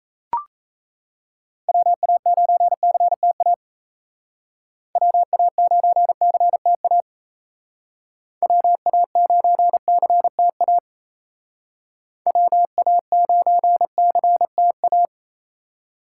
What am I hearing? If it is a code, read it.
Morse code, heard as a keyed tone, about 37 wpm, 700 Hz. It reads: WA9CTA WA9CTA WA9CTA WA9CTA